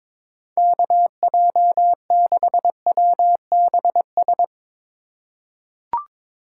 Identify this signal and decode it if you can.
Morse code, heard as a keyed tone, about 22 wpm, 700 Hz. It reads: KJ6WBS
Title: USA Call Signs - 3x Character Spacing - 22wpm